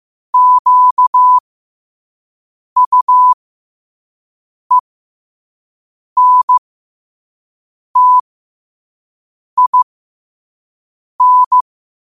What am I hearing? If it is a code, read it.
Morse code, heard as a keyed tone, about 15 wpm, spaced as for 5 wpm, 1000 Hz. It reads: QUENTIN
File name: qotdfiles / 5wpm